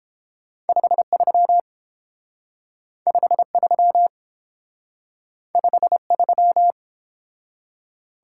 Morse code, heard as a keyed tone, about 29 wpm, 700 Hz. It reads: 53 53 53